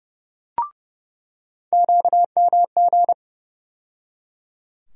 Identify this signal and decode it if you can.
Morse code, heard as a keyed tone, about 30 wpm, 700 Hz. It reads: QMG
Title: USA Call Sign Postfixes - 3 Letters 30wpm